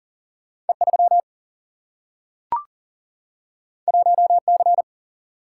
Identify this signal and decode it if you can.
Morse code, heard as a keyed tone, about 39 wpm, 700 Hz. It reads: E 3 1 C
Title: Single-Letter Single-Number 40wpm